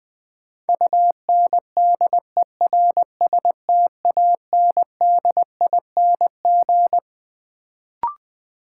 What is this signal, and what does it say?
Morse code, heard as a keyed tone, about 20 wpm, 700 Hz. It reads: UNDERSTANDING
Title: Sentences from Top 1000 Words - Review Part 3 - New Word Drill - 20wpm